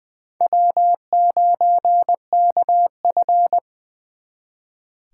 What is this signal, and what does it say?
Morse code, heard as a keyed tone, about 23 wpm, 700 Hz. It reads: W9KF
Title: Speed Racing - USA 1x2 Call Signs 20wpm